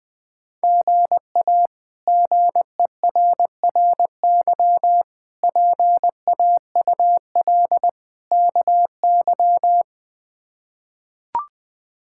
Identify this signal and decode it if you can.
Morse code, heard as a keyed tone, about 21 wpm, 700 Hz. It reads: GA GERRY PAUL KY
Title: K1USN SST QSO Practice - 20wpm